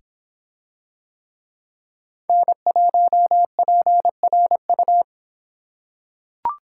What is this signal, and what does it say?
Morse code, heard as a keyed tone, about 23 wpm, 700 Hz. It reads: N1PRU